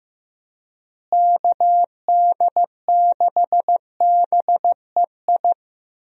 Morse code, heard as a keyed tone, about 15 wpm, 700 Hz. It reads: KD6BEI
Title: USA Call Signs - 6 Characters - Repeated 3x - 15wpm